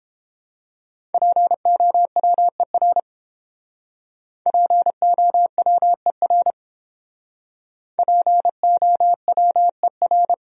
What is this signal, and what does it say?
Morse code, heard as a keyed tone, about 30 wpm, 700 Hz. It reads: POWER POWER POWER